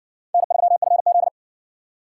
Morse code, heard as a keyed tone, about 56 wpm, 700 Hz. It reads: N3FZ